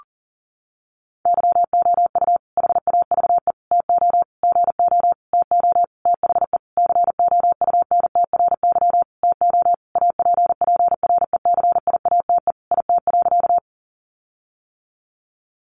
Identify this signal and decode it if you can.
Morse code, heard as a keyed tone, about 40 wpm, 700 Hz. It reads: YOU HAVE TO GO TO THE COUNTRY TO APPRECIATE IT.
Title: Sentences from Top 1300 Words 40wpm